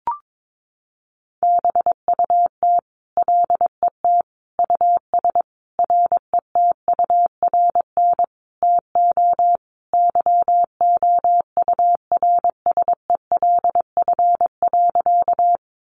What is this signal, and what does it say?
Morse code, heard as a keyed tone, about 22 wpm, 700 Hz. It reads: BUT LET US RETURN TO YOURSELF.
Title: Sentences from Top 800 Words 22wpm